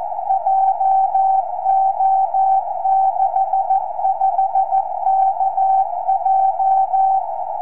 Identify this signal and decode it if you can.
Morse code, heard as a keyed tone, about 14 wpm, 770 Hz. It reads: JO65KJ
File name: tropo / sk7vhf